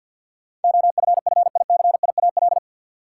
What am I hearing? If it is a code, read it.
Morse code, heard as a keyed tone, about 50 wpm, 700 Hz. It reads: OFFICIAL
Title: Sentences from Top 1000 Words - New Word Drill - 50wpm